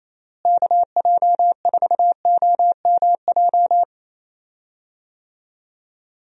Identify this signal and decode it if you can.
Morse code, heard as a keyed tone, about 28 wpm, 700 Hz. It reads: KJ4OMJ
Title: USA Call Signs - 6 Characters - Repeated 2x - 28wpm